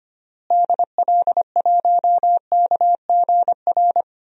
Morse code, heard as a keyed tone, about 25 wpm, 700 Hz. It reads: DL1KGR
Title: DXCC-100 Call Signs - 6 Characters - Repeated 3x - 25wpm